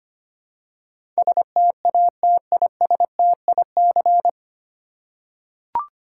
Morse code, heard as a keyed tone, about 25 wpm, 690 Hz. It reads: STATISTIC
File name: practice / 25wpm / 9 Letter Words - First 3 Letters Given - 25wpm